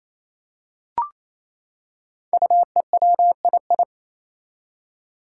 Morse code, heard as a keyed tone, about 27 wpm, 700 Hz. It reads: UEWII
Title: Five Challenging Letter-Number Combinations 28wpm